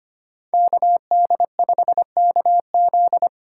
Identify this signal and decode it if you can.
Morse code, heard as a keyed tone, about 25 wpm, 700 Hz. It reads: KD5KZ
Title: USA Call Signs - 4x Character Spacing - 25wpm